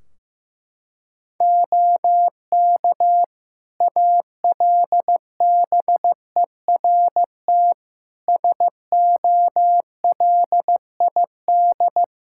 Morse code, heard as a keyed tone, about 15 wpm, 700 Hz. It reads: OK ALBERT SOLID
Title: Practice QSO - Exchange 3 - More Info - 15wpm